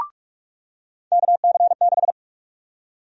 Morse code, heard as a keyed tone, about 46 wpm, 700 Hz. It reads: KC6
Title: USA Call Sign Prefixes - 45wpm